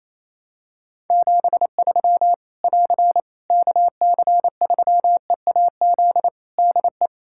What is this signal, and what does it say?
Morse code, heard as a keyed tone, about 28 wpm, 700 Hz. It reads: 73 <AR> KC3EAZ DE
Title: Practice QSO - Exchange 4 - Ending - 28wpm